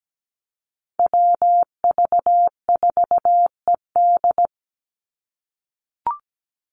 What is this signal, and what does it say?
Morse code, heard as a keyed tone, about 17 wpm, 700 Hz. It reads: WV4ED
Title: USA Call Signs 17wpm